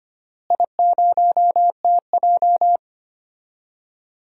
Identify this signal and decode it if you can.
Morse code, heard as a keyed tone, about 25 wpm, 700 Hz. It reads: I0TJ
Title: DXCC-100 Call Signs - 4 Characters - Repeated 3x - 25wpm